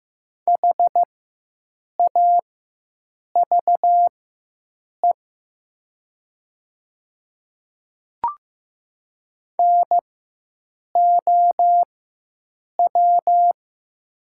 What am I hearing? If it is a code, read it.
Morse code, heard as a keyed tone, about 15 wpm, spaced as for 7 wpm, 700 Hz. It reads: HAVE NOW
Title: 4x Character Spacing - Top 100 Words - 15wpm